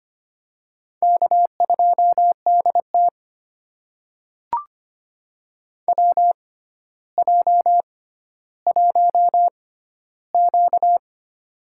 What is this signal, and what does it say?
Morse code, heard as a keyed tone, about 25 wpm, spaced as for 13 wpm, 700 Hz. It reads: K2DT WJ1Q
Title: USA Call Signs - 4 Characters - 6x Character Spacing - 25wpm